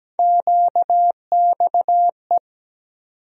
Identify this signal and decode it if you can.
Morse code, heard as a keyed tone, about 17 wpm, 700 Hz. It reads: QXE
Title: USA Call Sign Postfixes - 3 Letters 17wpm